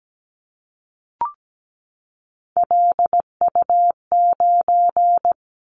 Morse code, heard as a keyed tone, about 17 wpm, 700 Hz. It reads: LU9
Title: Callsign Prefixes for Top 100 DXCC - 17wpm